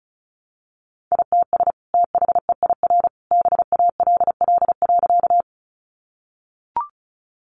Morse code, heard as a keyed tone, about 34 wpm, 700 Hz. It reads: ITS THEIR BALL.
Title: Sentences from Top 900 Words 35wpm